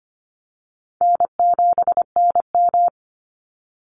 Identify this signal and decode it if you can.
Morse code, heard as a keyed tone, about 25 wpm, 700 Hz. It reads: N7NM
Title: Easy Call Signs - 25wpm